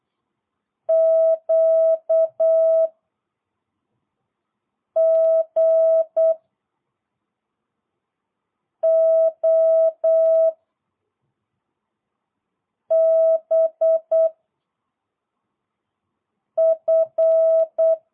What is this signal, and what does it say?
Morse code, heard as a keyed tone, about 8 wpm, 650 Hz. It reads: Q G O B F